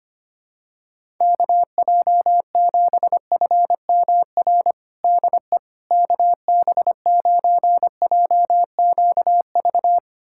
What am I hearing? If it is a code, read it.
Morse code, heard as a keyed tone, about 25 wpm, 700 Hz. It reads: KJ7FMR DE KB9JQV
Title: USA Callsign DE Callsign 25wpm